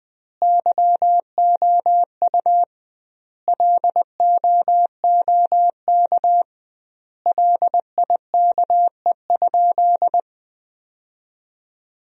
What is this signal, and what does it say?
Morse code, heard as a keyed tone, about 20 wpm, 700 Hz. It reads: YOU LOOK LIKE?
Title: 2x Word-Spacing - Sentences from Top 100 Words - 20wpm